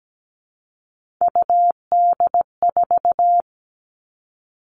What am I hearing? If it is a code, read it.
Morse code, heard as a keyed tone, about 17 wpm, 700 Hz. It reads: UD4
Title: All Cut Numbers - 3-Digit 17wpm